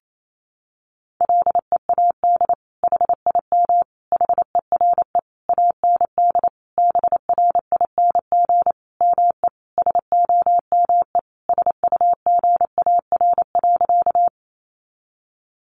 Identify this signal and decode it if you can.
Morse code, heard as a keyed tone, about 28 wpm, 700 Hz. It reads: LEAD HIM HERE AND BRING ME SOME SUGAR.
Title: Sentences from Top 1200 Words 28wpm